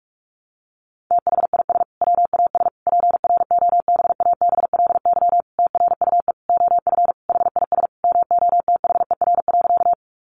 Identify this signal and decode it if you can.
Morse code, heard as a keyed tone, about 44 wpm, 700 Hz. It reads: THIS WAS PROBABLY TRUE OF HIS MOTHER.
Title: Sentences from Top 700 Words 45wpm